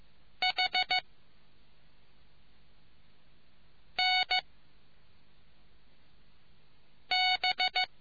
Morse code, heard as a keyed tone, about 15 wpm, 740 Hz. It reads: H N B